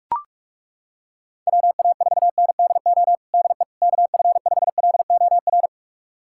Morse code, heard as a keyed tone, about 41 wpm, 690 Hz. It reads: WA4NDO DE KF5LOR
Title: USA Callsign DE Callsign 45wpm